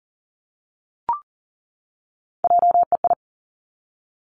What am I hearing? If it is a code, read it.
Morse code, heard as a keyed tone, about 42 wpm, 700 Hz. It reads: JEI
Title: Three Challenging Letter-Number Combinations 40wpm